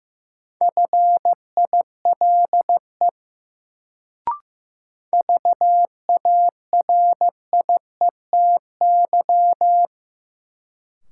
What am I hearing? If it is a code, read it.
Morse code, heard as a keyed tone, about 15 wpm, 700 Hz. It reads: FILE VARIETY
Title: Sentences from Top 1000 Words - Review Part 4 - New Word Drill - 15wpm